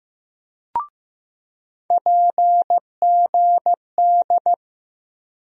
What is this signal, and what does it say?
Morse code, heard as a keyed tone, about 15 wpm, 700 Hz. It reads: PGD
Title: USA Call Sign Postfixes - 3 Letters 15wpm